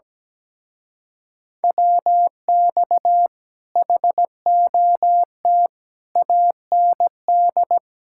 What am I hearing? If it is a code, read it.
Morse code, heard as a keyed tone, about 17 wpm, 700 Hz. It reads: WX HOT AND
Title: Speed-Racing Common QSO Phrases - 17wpm